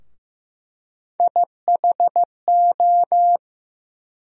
Morse code, heard as a keyed tone, about 15 wpm, 700 Hz. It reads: IHO
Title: USA Call Sign Postfixes - 3 Letters 15wpm